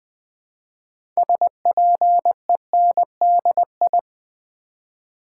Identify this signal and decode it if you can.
Morse code, heard as a keyed tone, about 20 wpm, 700 Hz. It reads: SPENDI
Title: Build A Word - 3 Letters At A Time - 8-Letter Words - 20wpm